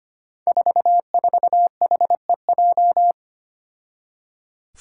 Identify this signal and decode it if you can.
Morse code, heard as a keyed tone, about 25 wpm, 700 Hz. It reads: 44HEJ
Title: Five Challenging Letter-Number Combinations 25wpm